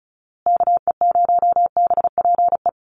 Morse code, heard as a keyed tone, about 34 wpm, 700 Hz. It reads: KE0BPE
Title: USA Call Signs Repeated 3x 35wpm